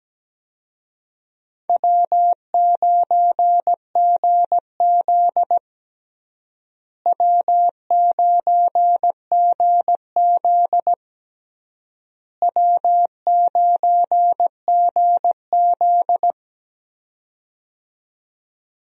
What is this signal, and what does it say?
Morse code, heard as a keyed tone, about 17 wpm, 700 Hz. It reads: W9GZ W9GZ W9GZ
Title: USA Call Signs - 4 Characters - Repeated 3x - 17wpm